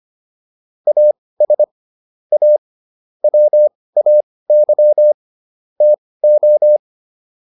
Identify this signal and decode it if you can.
Morse code, heard as a keyed tone, about 25 wpm, spaced as for 18 wpm, 600 Hz. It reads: AS A WAY TO